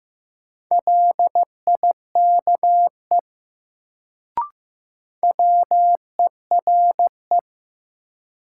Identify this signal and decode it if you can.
Morse code, heard as a keyed tone, about 15 wpm, 700 Hz. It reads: LIKE WERE
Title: Sentences from Top 100 Words - New Word Drill - 15wpm